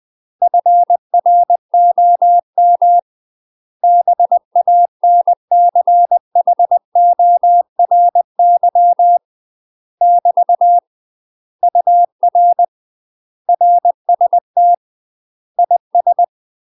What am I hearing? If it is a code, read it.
Morse code, heard as a keyed tone, about 20 wpm, 700 Hz. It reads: FROM BANCHORY <BT> UR RST IS
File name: UK MASTER QSOs TRAINING / 20WPM / UK MASTER QSOs 0023